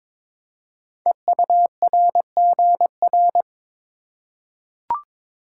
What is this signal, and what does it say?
Morse code, heard as a keyed tone, about 23 wpm, 700 Hz. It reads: EURGR
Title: Code Groups - 5 Letters - 22wpm